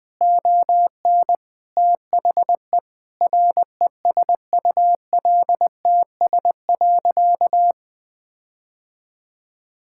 Morse code, heard as a keyed tone, about 20 wpm, 700 Hz. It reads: ON THE RESULTS.
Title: Sentences with 7-Word Phrases - 20wpm